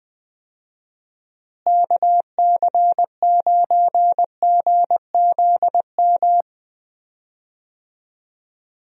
Morse code, heard as a keyed tone, about 20 wpm, 700 Hz. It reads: KC9GZM
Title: USA Call Signs - 6 Characters - Repeated 3x - 20wpm